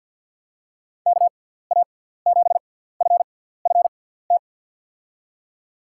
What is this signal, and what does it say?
Morse code, heard as a keyed tone, about 48 wpm, spaced as for 17 wpm, 700 Hz. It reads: KA7FFT
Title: USA Call Signs - 6 Characters - 6x Character Spacing - 50wpm